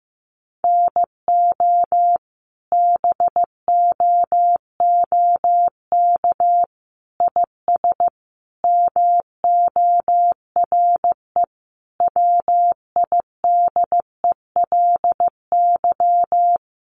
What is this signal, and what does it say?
Morse code, heard as a keyed tone, about 14 wpm, 700 Hz. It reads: NO BOOK IS MORE WIDELY
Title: Sentences from Top 1200 Words 15wpm